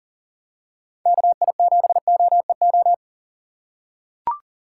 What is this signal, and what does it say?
Morse code, heard as a keyed tone, about 40 wpm, 700 Hz. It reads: KI7OEO
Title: USA Call Signs - 6 Characters - 6x Character Spacing - 40wpm